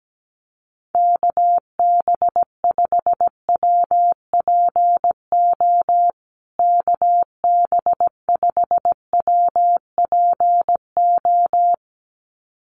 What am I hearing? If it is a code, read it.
Morse code, heard as a keyed tone, about 17 wpm, 700 Hz. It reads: KB5WPO KB5WPO
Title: USA Call Signs Repeated 2x 17wpm